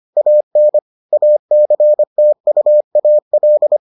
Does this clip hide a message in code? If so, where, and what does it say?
Yes: Morse code from 0.2 to 3.8 s, heard as a keyed tone, about 25 wpm, 600 Hz. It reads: AN ACTUAL